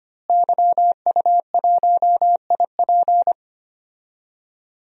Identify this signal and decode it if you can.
Morse code, heard as a keyed tone, about 25 wpm, 700 Hz. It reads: YU1IP
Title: DXCC-100 Call Signs - 5 Characters - Repeated 3x - 25wpm